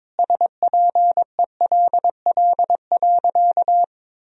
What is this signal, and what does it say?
Morse code, heard as a keyed tone, about 21 wpm, 700 Hz. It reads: SPELL.
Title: Sentences from Top 1000 Words - Review Part 1 - 22wpm